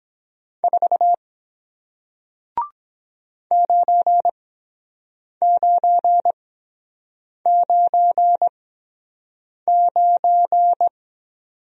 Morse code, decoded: 4 9999

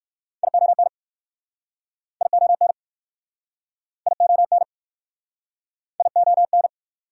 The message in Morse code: ION ION ION ION

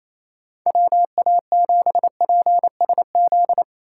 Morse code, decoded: WA7PSZ

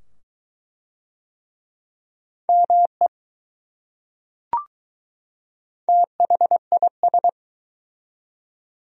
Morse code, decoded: ME THIS

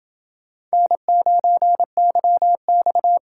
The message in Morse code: N9YX

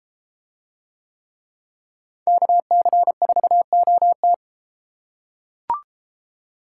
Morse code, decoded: KC4OT

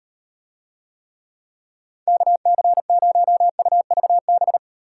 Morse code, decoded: KC0UVB

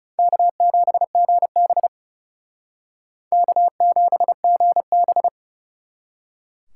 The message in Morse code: K7GB K7GB

